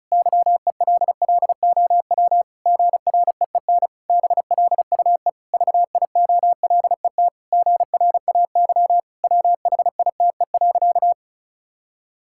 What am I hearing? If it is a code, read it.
Morse code, heard as a keyed tone, about 34 wpm, 700 Hz. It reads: YELLOW GREEN BLUE VIOLET GRAY WHITE.
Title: Pangram Challenge v2 - 35wpm